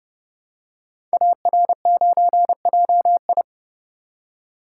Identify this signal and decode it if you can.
Morse code, heard as a keyed tone, about 30 wpm, 700 Hz. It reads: AR9JI
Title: Code Groups - 5 Letter:Numbers - 30wpm